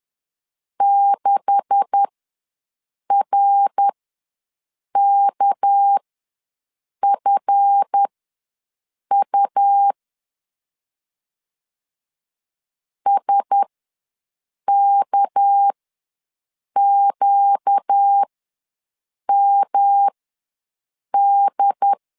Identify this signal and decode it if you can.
Morse code, heard as a keyed tone, about 10 wpm, spaced as for 6 wpm, 800 Hz. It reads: 6RKFU SKQMD
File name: LEZIONI / 9 286 ws 10wpm 4032